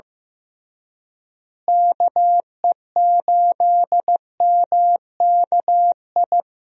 Morse code, heard as a keyed tone, about 16 wpm, 700 Hz. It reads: KE8MKI
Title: USA Call Signs - 6 Characters - Repeated 3x - 15wpm